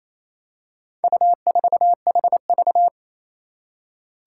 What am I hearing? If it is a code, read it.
Morse code, heard as a keyed tone, about 28 wpm, 700 Hz. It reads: U4HV
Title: Four Challenging Letter-Number Combinations 28wpm